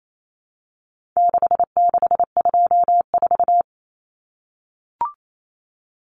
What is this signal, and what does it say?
Morse code, heard as a keyed tone, about 28 wpm, 700 Hz. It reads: 6624
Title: Numbers - 4-Digit 28wpm